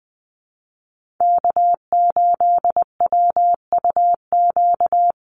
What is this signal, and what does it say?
Morse code, heard as a keyed tone, about 20 wpm, 700 Hz. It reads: K8WUQ
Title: USA Call Signs Repeated 3x 20wpm